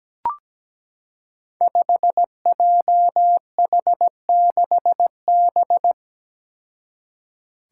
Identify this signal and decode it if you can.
Morse code, heard as a keyed tone, about 17 wpm, 700 Hz. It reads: 5JH6B
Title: Five Challenging Letter-Number Combinations 17wpm